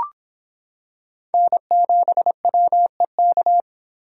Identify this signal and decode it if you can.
Morse code, heard as a keyed tone, about 25 wpm, 700 Hz. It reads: N7WEK